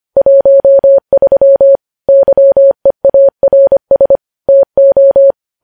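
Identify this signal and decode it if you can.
Morse code, heard as a keyed tone, about 25 wpm, 550 Hz. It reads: 13 YEARS TO
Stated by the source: Morse news headlines sent at 25 wpm.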